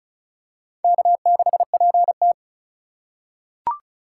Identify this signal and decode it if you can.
Morse code, heard as a keyed tone, about 36 wpm, 700 Hz. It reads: K6PT
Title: USA Call Signs - 4 Characters - 4x Character Spacing - 35wpm